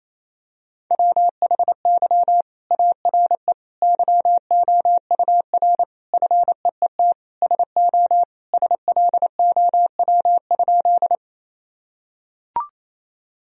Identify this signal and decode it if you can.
Morse code, heard as a keyed tone, about 28 wpm, 700 Hz. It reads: WHY ARE YOUR FEET SO SLOW?
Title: Sentences from Top 1000 Words - Review Part 2 - 28wpm